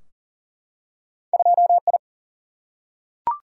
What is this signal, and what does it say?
Morse code, heard as a keyed tone, about 38 wpm, 700 Hz. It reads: 2I
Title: Single-Letter Single-Number 40wpm